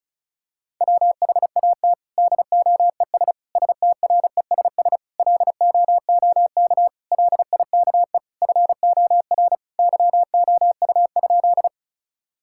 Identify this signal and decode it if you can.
Morse code, heard as a keyed tone, about 36 wpm, 700 Hz. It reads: WHAT DOES STRESS LOOK LIKE FOR YOU?
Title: Sentences from Top 1000 Words - Review Part 6 - 35wpm